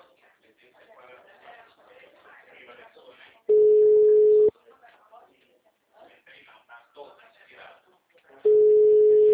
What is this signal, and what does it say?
Ringback tone (Europe, 425 Hz), with speech.
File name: out- -2016- - - .21302